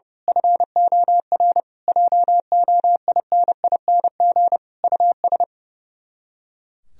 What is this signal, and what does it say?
Morse code, heard as a keyed tone, about 30 wpm, 700 Hz. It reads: FOR JOINING US
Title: Speed-Racing - Sets of 5 Words 30wpm